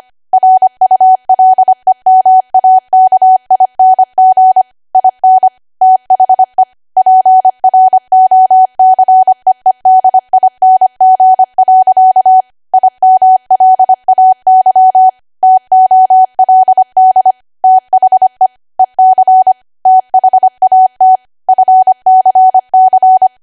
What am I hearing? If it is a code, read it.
Morse code, heard as a keyed tone, about 25 wpm, 750 Hz. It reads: RULEMAKING IN THE PROCEEDING. IMLAY TOLD THE EC THAT FCC
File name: cwa / W1AW-Level3 / 25WPM